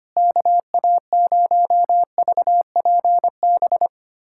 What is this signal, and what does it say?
Morse code, heard as a keyed tone, about 25 wpm, 700 Hz. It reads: KA0VPB
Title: USA Callsign DE Callsign - 6 Characters - 25wpm